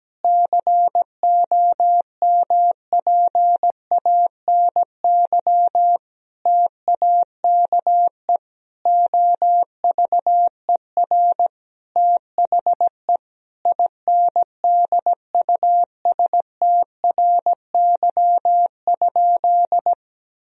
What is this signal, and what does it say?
Morse code, heard as a keyed tone, about 17 wpm, 700 Hz. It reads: COMPANY TAKE OVER THE INDUSTRY?
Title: Sentences from Top 1000 Words - Review Part 4 - 17wpm